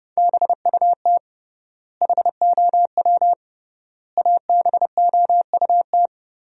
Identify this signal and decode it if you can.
Morse code, heard as a keyed tone, about 32 wpm, 700 Hz. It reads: BUT HOW ABOUT
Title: 3x Word-Spacing - Sentences from Top 100 Words - 30wpm